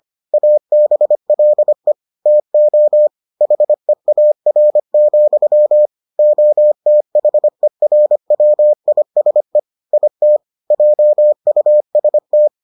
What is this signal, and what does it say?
Morse code, heard as a keyed tone, about 25 wpm, 600 Hz. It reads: ABLE TO HEAR, OTHERWISE IT JUST